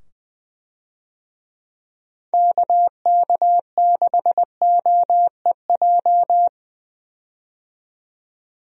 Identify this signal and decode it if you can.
Morse code, heard as a keyed tone, about 20 wpm, 700 Hz. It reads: KK6OEJ
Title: USA Call Signs - 6 Characters - 20wpm